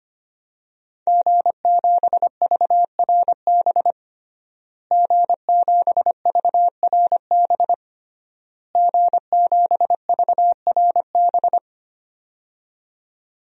Morse code, heard as a keyed tone, about 25 wpm, 700 Hz. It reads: G7VRB G7VRB G7VRB
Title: DXCC-100 Call Signs - 5 Characters - Repeated 3x - 25wpm